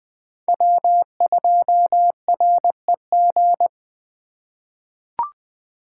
Morse code, heard as a keyed tone, about 20 wpm, 700 Hz. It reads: W2REG
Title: USA Call Signs - 5 Characters - 2x Character Spacing - 20wpm